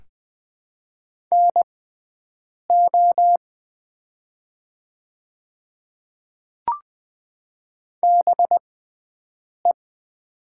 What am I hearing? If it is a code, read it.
Morse code, heard as a keyed tone, about 20 wpm, spaced as for 7 wpm, 700 Hz. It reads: NO BE
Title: 6x Character Spacing - Top 200 Words - 20wpm